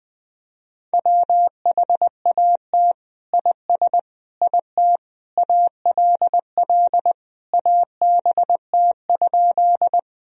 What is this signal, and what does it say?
Morse code, heard as a keyed tone, about 21 wpm, 700 Hz. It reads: WHAT IS IT ALL ABT?
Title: Sentences from Top 100 Words and Abbreviations 20wpm